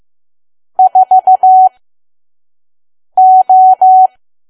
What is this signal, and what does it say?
Morse code, heard as a keyed tone, about 15 wpm, spaced as for 5 wpm, 750 Hz. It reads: 4O
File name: Archive / 5 WPM files / PM